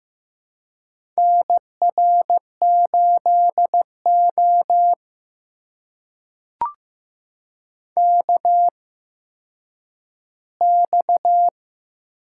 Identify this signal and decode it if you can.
Morse code, heard as a keyed tone, about 15 wpm, spaced as for 9 wpm, 700 Hz. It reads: NR8O KX
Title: USA Call Signs - 4 Characters - 8x Character Spacing - 15wpm